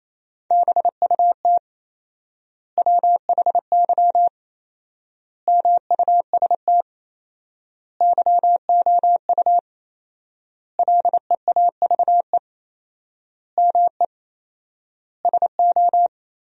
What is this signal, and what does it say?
Morse code, heard as a keyed tone, about 28 wpm, 700 Hz. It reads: BUT WHY MUST YOU LEAVE ME SO
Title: 4x Word-Spacing - Sentences from Top 400 Words - 28wpm